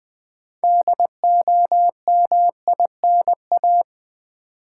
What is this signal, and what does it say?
Morse code, heard as a keyed tone, about 20 wpm, 700 Hz. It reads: DOMINA
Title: Build A Word - 3 Letters At A Time - 9-Letter Words - 20wpm